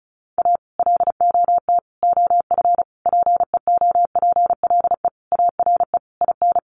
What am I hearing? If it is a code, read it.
Morse code, heard as a keyed tone, about 34 wpm, 700 Hz. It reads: A LOT OF PEOPLE ARE IN
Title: Additional Easy Phrases 35wpm